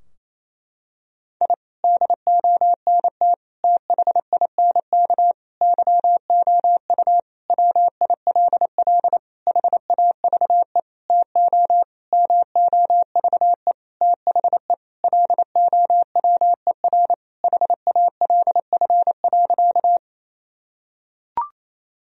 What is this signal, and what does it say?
Morse code, heard as a keyed tone, about 28 wpm, 700 Hz. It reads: I DONT THINK YOU WILL HAVE TO MOVE THE LOWER HALF.
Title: Sentences from Top 1000 Words - Review Part 4 - 28wpm